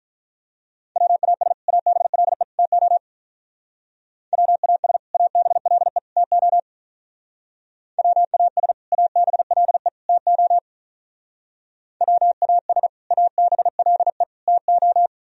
Morse code, heard as a keyed tone, about 42 wpm, 700 Hz. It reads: WAS ABLE TO WAS ABLE TO WAS ABLE TO WAS ABLE TO
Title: Speed-Racing - Sets of 3 Words 35wpm